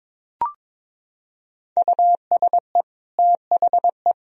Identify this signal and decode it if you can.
Morse code, heard as a keyed tone, about 22 wpm, 700 Hz. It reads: USE THE
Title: Sentences from Top 1000 Words - Review Part 2 - 22wpm